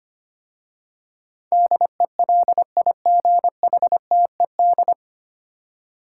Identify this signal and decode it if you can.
Morse code, heard as a keyed tone, about 25 wpm, 700 Hz. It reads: DELIGHTED
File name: practice / 25wpm / 9 Letter Words - First 3 Letters Given - 25wpm